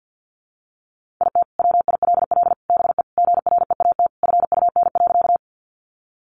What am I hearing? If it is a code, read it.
Morse code, heard as a keyed tone, about 50 wpm, 700 Hz. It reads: IT WILL BE GREAT FUN.
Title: Sentences from Top 700 Words 50wpm